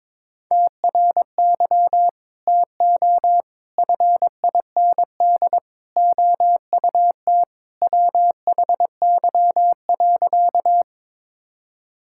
Morse code, heard as a keyed tone, about 21 wpm, 700 Hz. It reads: TRY TO FIND OUT WHY.
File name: practice / 22wpm / Sentences with 3-Word Phrases - 22wpm